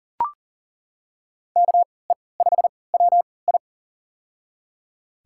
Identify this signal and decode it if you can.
Morse code, heard as a keyed tone, about 42 wpm, spaced as for 23 wpm, 700 Hz. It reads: KE5WI
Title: USA Call Signs - 5 Characters - 3x Character Spacing - 40wpm